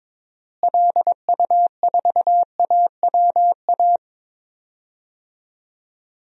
Morse code, heard as a keyed tone, about 22 wpm, 700 Hz. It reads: LU4AWA
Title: DXCC-100 Call Signs - 6 Characters - Repeated 3x - 22wpm